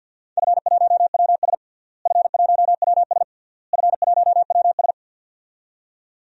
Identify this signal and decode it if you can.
Morse code, heard as a keyed tone, about 47 wpm, 700 Hz. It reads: F1WS F1WS F1WS
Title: DXCC-100 Call Signs - 4 Characters - Repeated 3x - 50wpm